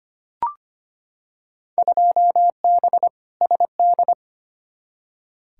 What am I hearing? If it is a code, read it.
Morse code, heard as a keyed tone, about 25 wpm, 700 Hz. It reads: 2B SD
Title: Field Day Exchanges Encore - 25wpm